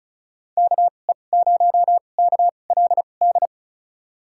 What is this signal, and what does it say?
Morse code, heard as a keyed tone, about 34 wpm, spaced as for 25 wpm, 700 Hz. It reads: KE0KLD
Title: USA Call Signs - 6 Characters - 2x Character Spacing - 35wpm